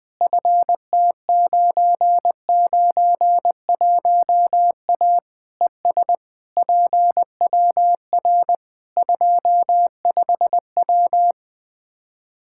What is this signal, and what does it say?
Morse code, heard as a keyed tone, about 21 wpm, 700 Hz. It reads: FT991A ES PWR 25W
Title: Practice QSO - Exchange 2 - Info - 20wpm